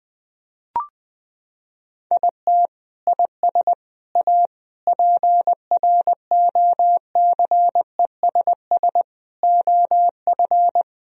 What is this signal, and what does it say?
Morse code, heard as a keyed tone, about 20 wpm, 700 Hz. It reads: IT IS A PROCESS OF
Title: Sentences from Top 1000 Words - Review Part 4 - 20wpm